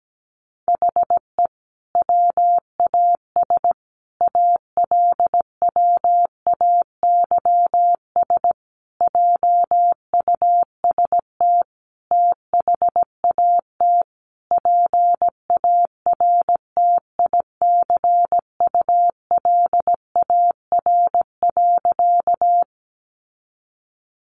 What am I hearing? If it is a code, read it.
Morse code, heard as a keyed tone, about 17 wpm, 700 Hz. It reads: HE WAS ALWAYS JUST THAT PARTICULAR.
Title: Sentences from Top 600 Words 17wpm